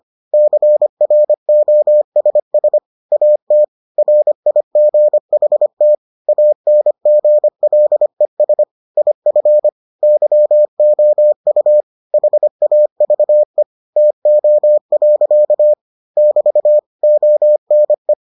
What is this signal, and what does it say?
Morse code, heard as a keyed tone, about 25 wpm, 600 Hz. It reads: CROSS AT RIGHT ANGLES IF YOU HAVE TO. <BT> ONE